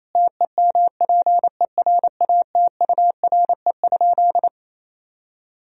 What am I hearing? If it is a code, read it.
Morse code, heard as a keyed tone, about 28 wpm, 700 Hz. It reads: TEMPERATURE?
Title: Sentences from Top 1000 Words - Review Part 4 - 28wpm